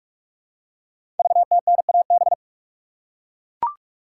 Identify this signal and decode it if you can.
Morse code, heard as a keyed tone, about 44 wpm, 700 Hz. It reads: VTNAB
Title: Code Groups - 5 Letters - 45wpm